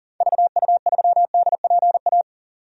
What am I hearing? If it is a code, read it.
Morse code, heard as a keyed tone, about 40 wpm, 700 Hz. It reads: VU3DPA
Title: DXCC-100 Call Signs - 6 Characters - 40wpm